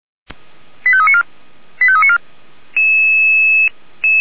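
Five-tone selective call (ZVEI), heard as quick runs of five tones, tones of about 80 ms. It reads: 85394 85394
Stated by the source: ZVEI five-tone selective call 89484 received by SDR.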